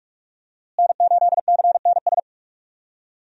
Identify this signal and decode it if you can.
Morse code, heard as a keyed tone, about 43 wpm, 700 Hz. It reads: N8CNS
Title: USA Call Signs - 5 Characters - Repeated 3x - 45wpm